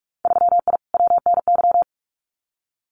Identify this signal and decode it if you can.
Morse code, heard as a keyed tone, about 44 wpm, 700 Hz. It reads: 3I WNY
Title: Winter Field Day Exchanges 45wpm